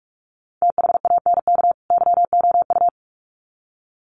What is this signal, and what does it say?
Morse code, heard as a keyed tone, about 44 wpm, 700 Hz. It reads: THANK YOU
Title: Sets of 2 Words 45wpm